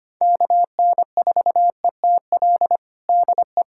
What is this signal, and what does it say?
Morse code, heard as a keyed tone, about 25 wpm, 700 Hz. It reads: KN4ETL DE
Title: USA Callsign DE Callsign 25wpm